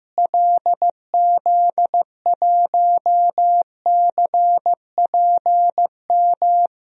Morse code, heard as a keyed tone, about 15 wpm, 700 Hz. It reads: LZ1CPM
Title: DXCC-100 Call Signs - 6 Characters - Repeated 3x - 15wpm